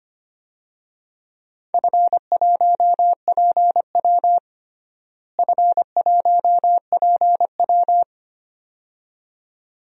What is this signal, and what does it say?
Morse code, heard as a keyed tone, about 25 wpm, 700 Hz. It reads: F1PW F1PW
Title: DXCC-100 Call Signs - 4 Characters - Repeated 2x - 25wpm